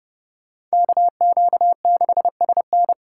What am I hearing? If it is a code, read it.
Morse code, heard as a keyed tone, about 37 wpm, 700 Hz. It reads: KQ6SN
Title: Speed Racing - USA 2x2 Call Signs 30wpm